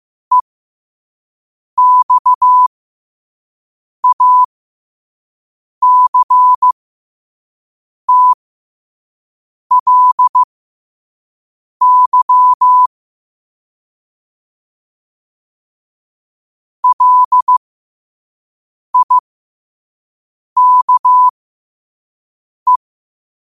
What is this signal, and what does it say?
Morse code, heard as a keyed tone, about 15 wpm, spaced as for 5 wpm, 1000 Hz. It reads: EXACTLY LIKE
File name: qotdfiles / 5wpm